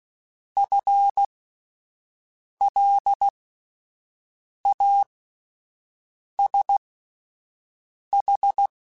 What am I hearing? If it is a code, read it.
Morse code, heard as a keyed tone, about 16 wpm, spaced as for 5 wpm, 800 Hz. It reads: FLASH
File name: Spaced / FL Runs sp mp3s / FL Runs sp @10